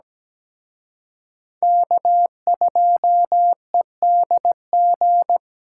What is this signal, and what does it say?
Morse code, heard as a keyed tone, about 20 wpm, 700 Hz. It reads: K2EDG
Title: Speed Racing - USA 1x3 Call Signs 17wpm